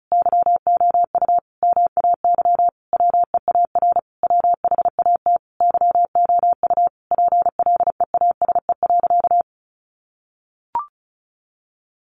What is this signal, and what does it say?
Morse code, heard as a keyed tone, about 34 wpm, 700 Hz. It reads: YOU MAY WEAR WHAT YOU PLEASE.
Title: Sentences from Top 1500 Words 35wpm